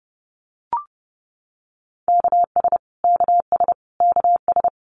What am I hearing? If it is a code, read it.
Morse code, heard as a keyed tone, about 30 wpm, 700 Hz. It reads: KS KS KS
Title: US States Repeated 3x 30wpm